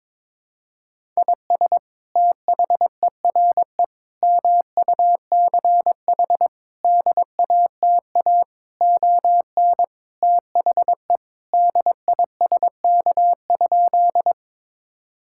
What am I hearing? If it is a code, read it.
Morse code, heard as a keyed tone, about 22 wpm, 700 Hz. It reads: IS THERE MUCH DATA ON THE DISK?
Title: Sentences from Top 1000 Words - Review Part 6 - 22wpm